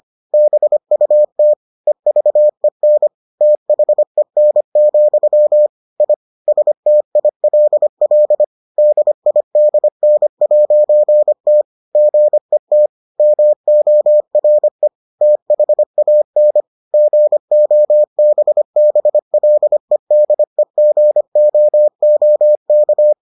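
Morse code, heard as a keyed tone, about 25 wpm, 600 Hz. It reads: BUT EVEN THEN, I STILL DIDN'T GET MORE THAN GOBBLEDEGOOK
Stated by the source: Morse edition of the Foundations of Amateur Radio episodes.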